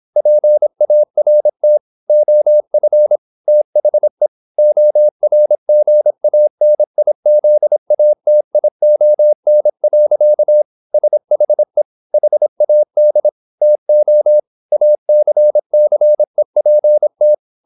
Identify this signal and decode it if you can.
Morse code, heard as a keyed tone, about 26 wpm, 600 Hz. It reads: PART OF THE ORGANIZATION. SHE HAD TO ACCEPT
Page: Never 26wpm | CW med Gnister